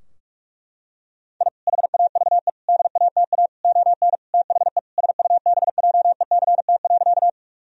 Morse code, heard as a keyed tone, about 46 wpm, 700 Hz. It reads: I HAVE DATA ON THE SUBJECT.
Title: Sentences from Top 1000 Words - Review Part 3 - 45wpm